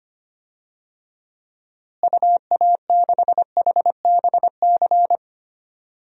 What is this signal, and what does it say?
Morse code, heard as a keyed tone, about 25 wpm, 700 Hz. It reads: UA6HBC
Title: DXCC-100 Call Signs - 6 Characters - Repeated 2x - 25wpm